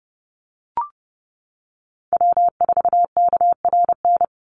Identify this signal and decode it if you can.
Morse code, heard as a keyed tone, about 30 wpm, 700 Hz. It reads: W4KRN
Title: NA SOTA Chaser Callsigns 30wpm